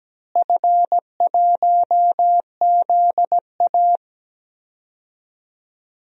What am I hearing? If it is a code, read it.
Morse code, heard as a keyed tone, about 17 wpm, 700 Hz. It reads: F1ZA
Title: DXCC-100 Call Signs - 4 Characters - Repeated 3x - 17wpm